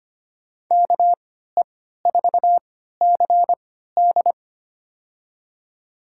Morse code, heard as a keyed tone, about 25 wpm, spaced as for 14 wpm, 700 Hz. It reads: KE4CD
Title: USA Call Signs - 5 Characters - 3x Character Spacing - 25wpm